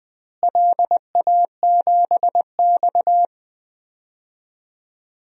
Morse code, heard as a keyed tone, about 21 wpm, 700 Hz. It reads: LA7X
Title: DXCC-100 Call Signs - 4 Characters - Repeated 2x - 20wpm